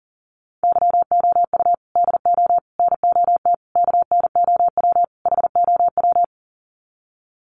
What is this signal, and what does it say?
Morse code, heard as a keyed tone, about 40 wpm, 700 Hz. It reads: YOU DO NOT KNOW HOW
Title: Sets of 5 Words Encore 40wpm